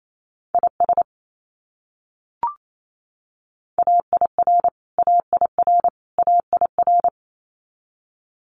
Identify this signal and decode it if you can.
Morse code, heard as a keyed tone, about 28 wpm, 700 Hz. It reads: IS AIR AIR AIR
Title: Top 300 Words Repeated 28wpm